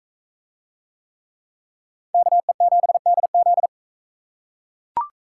Morse code, heard as a keyed tone, about 38 wpm, 700 Hz. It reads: KE7DZ